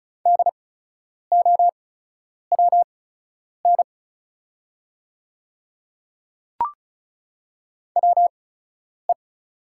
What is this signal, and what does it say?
Morse code, heard as a keyed tone, about 36 wpm, spaced as for 10 wpm, 700 Hz. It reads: DOWN WE
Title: 8x Character Spacing - Top 100 Words - 35wpm